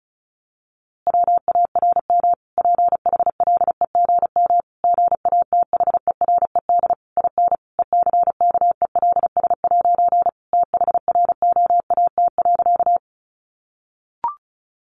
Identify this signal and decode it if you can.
Morse code, heard as a keyed tone, about 34 wpm, 700 Hz. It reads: WARM PHLEGM GATHERED IN ECKELS' THROAT.
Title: A Sound Of Thunder 35wpm